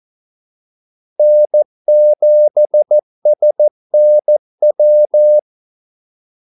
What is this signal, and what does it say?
Morse code, heard as a keyed tone, about 14 wpm, 600 Hz. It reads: N7SNW
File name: Callsigns-14wpm